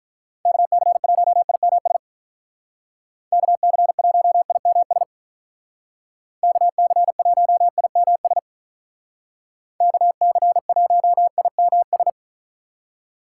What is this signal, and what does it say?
Morse code, heard as a keyed tone, about 43 wpm, 700 Hz. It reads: KC1IMS KC1IMS KC1IMS KC1IMS
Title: Speed Racing - USA 2x3 Call Signs 35wpm